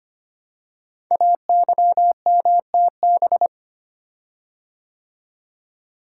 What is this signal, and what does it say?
Morse code, heard as a keyed tone, about 25 wpm, 700 Hz. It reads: AYMTB
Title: Code Groups - 5 Letters - 25wpm